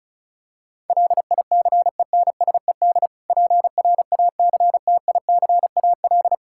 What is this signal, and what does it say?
Morse code, heard as a keyed tone, about 34 wpm, 700 Hz. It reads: LICENSED PRACTICAL
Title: Practice QSO - Exchange 3 - More Info - 35wpm